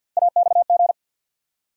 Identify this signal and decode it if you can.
Morse code, heard as a keyed tone, about 52 wpm, 700 Hz. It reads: AXG
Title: USA Call Sign Postfixes - 3 Letters 50wpm